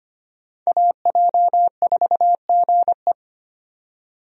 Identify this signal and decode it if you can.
Morse code, heard as a keyed tone, about 25 wpm, 690 Hz. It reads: AJ4GE
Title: USA Call Signs - 5 Characters - 25wpm